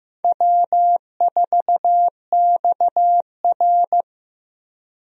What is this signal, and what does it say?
Morse code, heard as a keyed tone, about 15 wpm, 700 Hz. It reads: W4XR
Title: USA Call Signs - 4 Characters - 15wpm